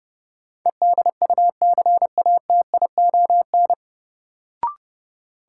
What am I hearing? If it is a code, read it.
Morse code, heard as a keyed tone, about 30 wpm, 700 Hz. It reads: EDUCATION
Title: Sentences from Top 1000 Words - Review Part 6 - New Word Drill - 30wpm